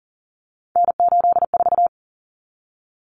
Morse code, heard as a keyed tone, about 38 wpm, 700 Hz. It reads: N84
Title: Common Cut Numbers - 3-Digit 40wpm